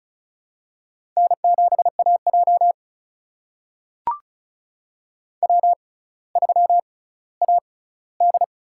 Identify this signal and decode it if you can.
Morse code, heard as a keyed tone, about 36 wpm, spaced as for 18 wpm, 700 Hz. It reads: N7AJ W3AD
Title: USA Call Signs - 6x Character Spacing - 35wpm